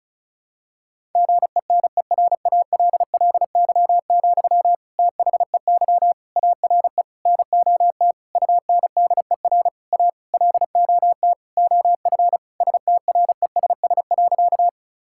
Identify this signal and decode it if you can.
Morse code, heard as a keyed tone, about 34 wpm, 700 Hz. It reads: GENERALLY, THEY ARE NOT UNDER A LOT OF STRESS.
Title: Sentences from Top 1000 Words - Review Part 6 - 35wpm